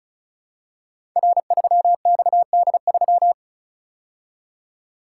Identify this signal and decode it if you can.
Morse code, heard as a keyed tone, about 36 wpm, 700 Hz. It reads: R3XD3
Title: Code Groups - 5 Letter:Numbers - 35wpm